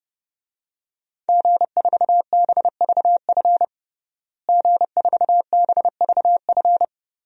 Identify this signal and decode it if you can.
Morse code, heard as a keyed tone, about 30 wpm, 700 Hz. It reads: G4BVF G4BVF
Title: DXCC-100 Call Signs - 5 Characters - Repeated 3x - 30wpm